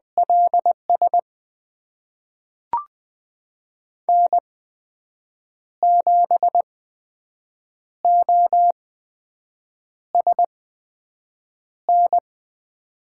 Morse code, heard as a keyed tone, about 20 wpm, spaced as for 6 wpm, 700 Hz. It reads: LS N7OSN
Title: USA Call Signs - 5 Characters - 8x Character Spacing - 20wpm